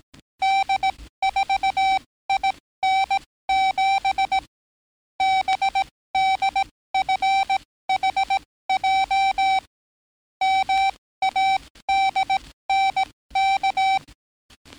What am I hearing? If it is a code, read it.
Morse code, heard as a keyed tone, about 18 wpm, spaced as for 15 wpm, 770 Hz. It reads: D4IN7 BDFHJ MADNK
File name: training / NAVPERS 11049to11060 / NAVPERS 11056 RA RB